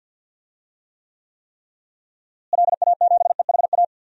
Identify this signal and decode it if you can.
Morse code, heard as a keyed tone, about 47 wpm, 700 Hz. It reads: LA7EHA
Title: DXCC-100 Call Signs - 6 Characters - Repeated 3x - 50wpm